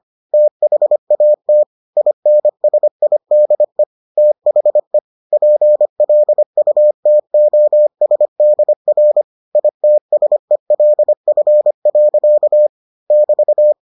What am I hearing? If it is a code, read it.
Morse code, heard as a keyed tone, about 25 wpm, 600 Hz. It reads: THAT INSIDE THE PLUTOSDR ITSELF. <BT>